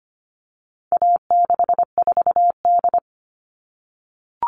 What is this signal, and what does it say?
Morse code, heard as a keyed tone, about 25 wpm, 700 Hz. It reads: A64D
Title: All Cut Numbers - 4-Digit 25wpm